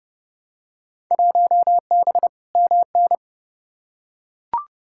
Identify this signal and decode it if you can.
Morse code, heard as a keyed tone, about 30 wpm, 700 Hz. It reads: 1B MN
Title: Field Day Exchanges - 30wpm